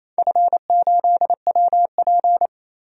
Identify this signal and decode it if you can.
Morse code, heard as a keyed tone, about 29 wpm, 700 Hz. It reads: F8WP
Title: DXCC-100 Call Signs - 4 Characters - 28wpm